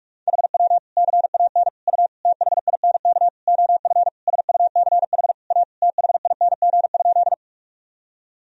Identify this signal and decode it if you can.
Morse code, heard as a keyed tone, about 41 wpm, 700 Hz. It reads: HW CAN U THINK OF SUCH A THING?
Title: Sentences from Top 200 Words and Abbreviations 45wpm